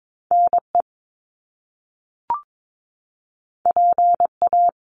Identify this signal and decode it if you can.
Morse code, heard as a keyed tone, about 22 wpm, 700 Hz. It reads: NE PA
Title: US States Repeated 3x 22wpm